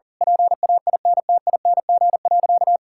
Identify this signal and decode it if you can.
Morse code, heard as a keyed tone, about 40 wpm, 700 Hz. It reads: PAINTING.
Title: Sentences from Top 1000 Words - Review Part 6 - 40wpm